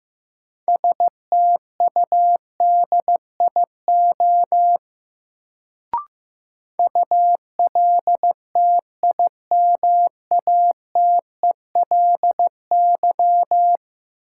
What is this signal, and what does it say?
Morse code, heard as a keyed tone, about 15 wpm, 700 Hz. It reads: STUDIO ULTIMATELY
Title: Sentences from Top 1200 Words - New Word Drill Alt - 15wpm